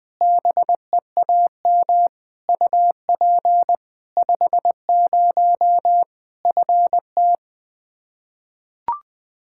Morse code, heard as a keyed tone, about 20 wpm, 700 Hz. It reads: BEAM UP 50 FT
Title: Practice QSO - Exchange 2 - Info - 20wpm